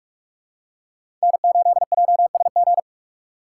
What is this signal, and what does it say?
Morse code, heard as a keyed tone, about 46 wpm, 700 Hz. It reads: N8JSG
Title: USA Call Signs - 5 Characters - 2x Character Spacing - 45wpm